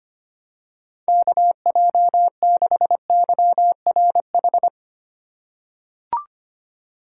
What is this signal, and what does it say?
Morse code, heard as a keyed tone, about 25 wpm, 700 Hz. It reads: KJ6YRH
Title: USA Call Signs - 6 Characters - 3x Character Spacing - 25wpm